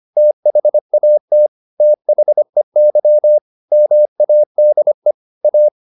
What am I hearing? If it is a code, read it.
Morse code, heard as a keyed tone, about 25 wpm, 600 Hz. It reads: THAT THEY MADE A
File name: the old man and the sea part 1 25wpm